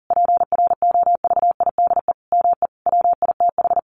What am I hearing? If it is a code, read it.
Morse code, heard as a keyed tone, about 40 wpm, 700 Hz. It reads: PROVIDE ME WITH
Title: Sentences from Top 1500 Words 40wpm